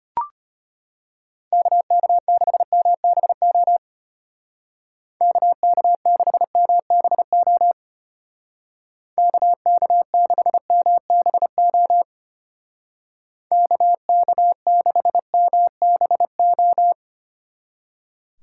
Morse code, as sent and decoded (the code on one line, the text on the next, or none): -.- -.- -.... -- -... --- / -.- -.- -.... -- -... --- / -.- -.- -.... -- -... --- / -.- -.- -.... -- -... ---
KK6MBO KK6MBO KK6MBO KK6MBO